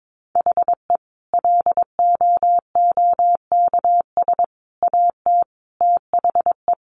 Morse code, heard as a keyed tone, about 22 wpm, 700 Hz. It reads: HE LOOKS AT THE
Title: Additional Easy Phrases 22wpm